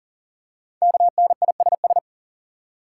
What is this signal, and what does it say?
Morse code, heard as a keyed tone, about 42 wpm, 700 Hz. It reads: KNISS
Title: Code Groups - 5 Letters - 40wpm